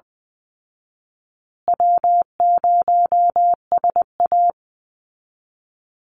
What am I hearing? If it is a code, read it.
Morse code, heard as a keyed tone, about 20 wpm, 700 Hz. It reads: W0SA
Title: Easy Call Signs Repeated 2x - 20wpm